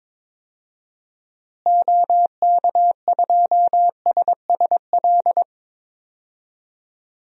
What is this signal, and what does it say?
Morse code, heard as a keyed tone, about 22 wpm, 700 Hz. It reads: OK2SSL
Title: DXCC-100 Call Signs - 6 Characters - Repeated 2x - 22wpm